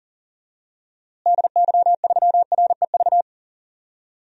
Morse code, heard as a keyed tone, about 40 wpm, 700 Hz. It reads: DY3REV
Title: DXCC-100 Call Signs - 6 Characters - Repeated 3x - 40wpm